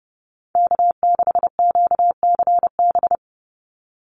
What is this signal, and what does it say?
Morse code, heard as a keyed tone, about 29 wpm, 700 Hz. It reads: K6QCB
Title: NA SOTA Chaser Callsigns 30wpm